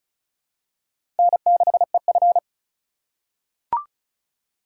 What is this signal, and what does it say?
Morse code, heard as a keyed tone, about 34 wpm, 700 Hz. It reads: N6EF
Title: USA Call Signs - 3x Character Spacing - 35wpm